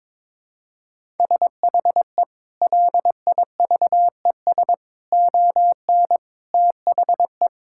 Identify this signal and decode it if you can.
Morse code, heard as a keyed tone, about 21 wpm, 700 Hz. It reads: SHE LIVES ON THE
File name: Sentences with 7-Word Phrases - 22wpm